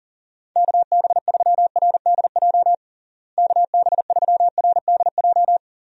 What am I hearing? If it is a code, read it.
Morse code, heard as a keyed tone, about 40 wpm, 700 Hz. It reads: KB3RDJ KB3RDJ
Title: USA Call Signs - 6 Characters - Repeated 2x - 40wpm